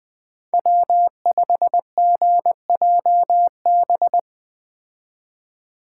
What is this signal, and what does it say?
Morse code, heard as a keyed tone, about 20 wpm, 700 Hz. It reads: W5GJB
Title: Speed Racing - USA 1x3 Call Signs 17wpm